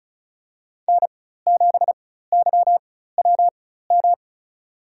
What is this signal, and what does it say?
Morse code, heard as a keyed tone, about 36 wpm, spaced as for 16 wpm, 700 Hz. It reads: N7YWM
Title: USA Call Signs - 4x Character Spacing - 35wpm